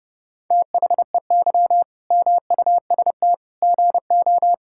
Morse code, heard as a keyed tone, about 30 wpm, 700 Hz. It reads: THEY MUST GO
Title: Sentences from Top 1600 Words 30wpm